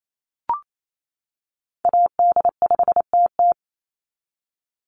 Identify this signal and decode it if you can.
Morse code, heard as a keyed tone, about 28 wpm, 700 Hz. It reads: AD5TT
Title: Call Signs - 28wpm